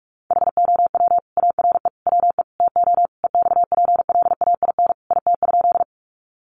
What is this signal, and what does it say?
Morse code, heard as a keyed tone, about 44 wpm, 700 Hz. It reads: HOW ARE WE TO EXPLAIN IT?
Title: Sentences from Top 1100 Words 45wpm